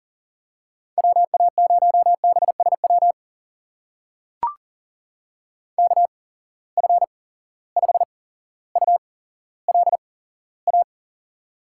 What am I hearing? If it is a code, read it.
Morse code, heard as a keyed tone, about 38 wpm, spaced as for 17 wpm, 700 Hz. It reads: WA0BSW KF5ULA
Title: USA Call Signs - 6 Characters - 8x Character Spacing - 40wpm